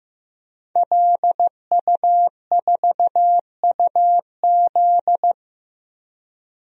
Morse code, heard as a keyed tone, about 15 wpm, 700 Hz. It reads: LU4UZ
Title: DXCC-100 Call Signs - 5 Characters - Repeated 2x - 15wpm